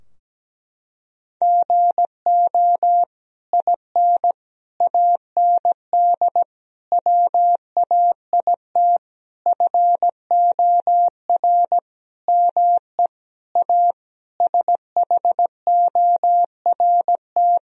Morse code, heard as a keyed tone, about 17 wpm, 700 Hz. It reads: GO IN AND WAIT FOR ME A SHORT
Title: Sentences from Top 1000 Words - Review Part 2 - 17wpm